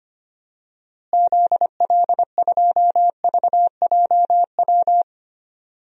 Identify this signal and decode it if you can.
Morse code, heard as a keyed tone, about 25 wpm, 700 Hz. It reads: ZL2VJW
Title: DXCC-100 Call Signs - 6 Characters - Repeated 3x - 25wpm